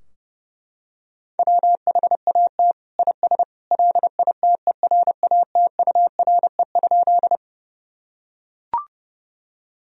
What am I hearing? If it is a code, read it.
Morse code, heard as a keyed tone, about 30 wpm, 700 Hz. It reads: WHAT IS LITERATURE?
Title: Sentences from Top 1000 Words - Review Part 3 - 30wpm